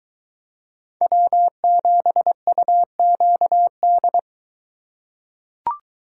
Morse code, heard as a keyed tone, about 23 wpm, 700 Hz. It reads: W7UQD